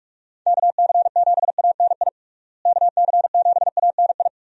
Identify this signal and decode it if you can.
Morse code, heard as a keyed tone, about 42 wpm, 700 Hz. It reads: KC7ANI KC7ANI
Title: USA Call Signs - 6 Characters - Repeated 3x - 45wpm